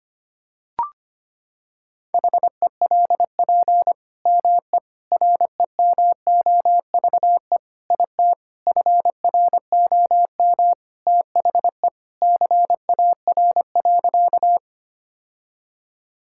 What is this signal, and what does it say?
Morse code, heard as a keyed tone, about 25 wpm, 700 Hz. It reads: HELP ME REMOVE IT FROM THE CAR.
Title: Sentences from Top 1000 Words - Review Part 5 - 25wpm